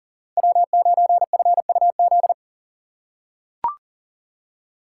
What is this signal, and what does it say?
Morse code, heard as a keyed tone, about 40 wpm, 700 Hz. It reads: W9FUZ
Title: USA Call Signs - 5 Characters - 3x Character Spacing - 40wpm